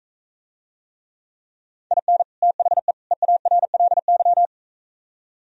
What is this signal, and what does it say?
Morse code, heard as a keyed tone, about 44 wpm, 700 Hz. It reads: IN THE EARLY